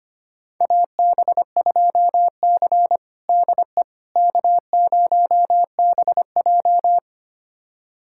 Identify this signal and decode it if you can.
Morse code, heard as a keyed tone, about 25 wpm, 700 Hz. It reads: AB2C DE K0BJ
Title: USA Callsign DE Callsign - 4 Characters - 25wpm